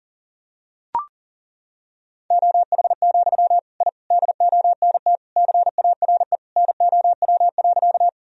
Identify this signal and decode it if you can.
Morse code, heard as a keyed tone, about 40 wpm, 700 Hz. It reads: OH, I DONT CARE NOW.
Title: Sentences from Top 1000 Words - Review Part 1 - 40wpm